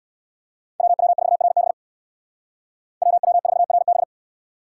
Morse code, heard as a keyed tone, about 59 wpm, 700 Hz. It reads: WW4RB WW4RB
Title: Speed Racing - USA 2x2 Call Signs 50wpm